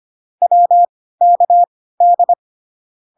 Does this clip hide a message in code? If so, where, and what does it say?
Yes: Morse code from 0.4 to 2.3 s, heard as a keyed tone, about 25 wpm, spaced as for 16 wpm, 700 Hz. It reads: WKD